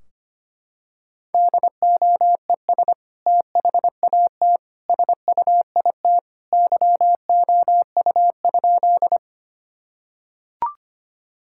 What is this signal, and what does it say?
Morse code, heard as a keyed tone, about 25 wpm, 700 Hz. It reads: DOES THAT SUIT YOU?
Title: Sentences from Top 1600 Words 25wpm